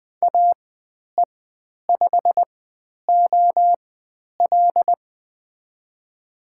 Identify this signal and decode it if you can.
Morse code, heard as a keyed tone, about 20 wpm, spaced as for 10 wpm, 700 Hz. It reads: AE5OL
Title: 095 Lesson 16g - Letter c - Call Signs 20wpm